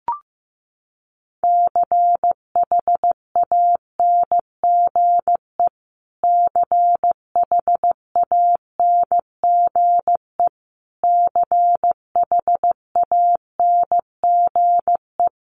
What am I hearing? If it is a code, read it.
Morse code, heard as a keyed tone, about 15 wpm, 700 Hz. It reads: CHANGE CHANGE CHANGE
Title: Top 200 Words Repeated 15wpm